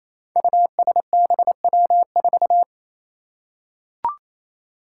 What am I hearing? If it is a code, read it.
Morse code, heard as a keyed tone, about 27 wpm, 700 Hz. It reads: USBW4
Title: Five Challenging Letter-Number Combinations 28wpm